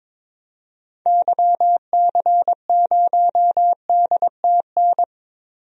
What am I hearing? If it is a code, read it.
Morse code, heard as a keyed tone, about 22 wpm, 700 Hz. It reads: YC0DTN
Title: DXCC-100 Call Signs - 6 Characters - Repeated 2x - 22wpm